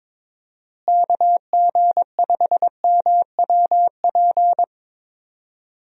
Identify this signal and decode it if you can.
Morse code, heard as a keyed tone, about 22 wpm, 700 Hz. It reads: KG5MWP
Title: USA Call Signs - 6 Characters - Repeated 3x - 22wpm